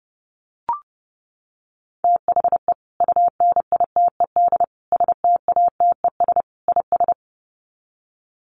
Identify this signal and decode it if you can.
Morse code, heard as a keyed tone, about 28 wpm, 700 Hz. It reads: THE UNITED STATES IS
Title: Sets of 4 Words 30wpm